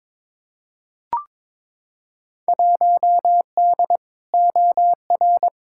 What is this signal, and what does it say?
Morse code, heard as a keyed tone, about 22 wpm, 700 Hz. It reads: 1D OR
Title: Field Day Exchanges - 22wpm